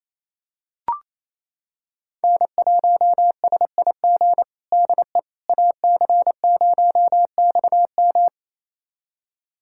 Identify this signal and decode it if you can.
Morse code, heard as a keyed tone, about 27 wpm, 700 Hz. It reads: N1SIG DE AC0XM
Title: USA Callsign DE Callsign - 5 Characters - 28wpm